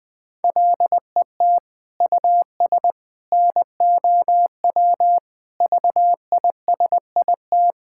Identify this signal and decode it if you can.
Morse code, heard as a keyed tone, about 20 wpm, 700 Hz. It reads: LET US NOW VISIT
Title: Sentences from Top 1000 Words - Review Part 6 - 20wpm